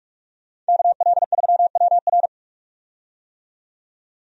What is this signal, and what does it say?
Morse code, heard as a keyed tone, about 44 wpm, 700 Hz. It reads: KL3WR
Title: USA Call Signs - 5 Characters - 45wpm